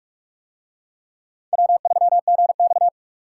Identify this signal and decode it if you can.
Morse code, heard as a keyed tone, about 44 wpm, 700 Hz. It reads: W3GX